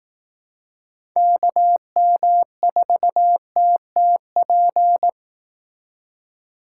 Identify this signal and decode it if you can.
Morse code, heard as a keyed tone, about 19 wpm, 700 Hz. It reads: KM4TTP
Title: Speed Racing - USA 2x3 Call Signs 15wpm